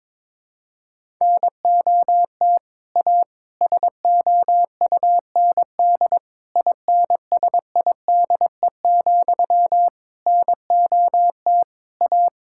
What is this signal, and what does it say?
Morse code, heard as a keyed tone, about 22 wpm, 700 Hz. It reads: NOT A SOUND INSIDE, NOT A
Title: Sentences from Top 1000 Words - Review Part 6 - 22wpm